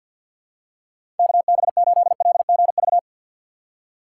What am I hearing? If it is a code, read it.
Morse code, heard as a keyed tone, about 53 wpm, 700 Hz. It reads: KB8LGV
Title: USA Call Signs - 6 Characters - 4x Character Spacing - 50wpm